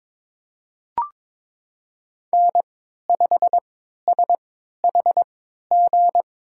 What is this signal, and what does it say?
Morse code, heard as a keyed tone, about 21 wpm, spaced as for 12 wpm, 700 Hz. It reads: N5SHG
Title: USA Call Signs - 5 Characters - 3x Character Spacing - 22wpm